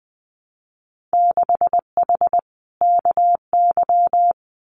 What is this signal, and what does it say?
Morse code, heard as a keyed tone, about 20 wpm, 700 Hz. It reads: 6H KY
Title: Winter Field Day Exchanges Encore 20wpm